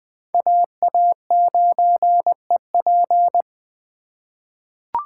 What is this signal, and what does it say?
Morse code, heard as a keyed tone, about 20 wpm, 700 Hz. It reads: AA9EP
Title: USA Call Signs - 5 Characters - 2x Character Spacing - 20wpm